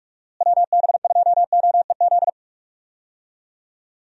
Morse code, heard as a keyed tone, about 45 wpm, 700 Hz. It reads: WB2OEZ
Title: USA Call Signs - 6 Characters - Repeated 2x - 45wpm